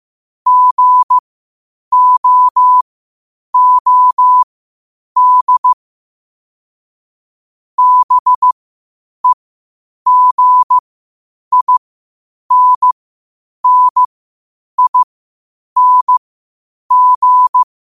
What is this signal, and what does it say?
Morse code, heard as a keyed tone, about 15 wpm, spaced as for 8 wpm, 1000 Hz. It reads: GOOD BEGINNING
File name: qotdfiles / 8wpm